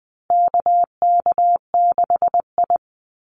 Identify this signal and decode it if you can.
Morse code, heard as a keyed tone, about 20 wpm, 700 Hz. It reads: KK6I
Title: Call Signs Repeated 2x - 20wpm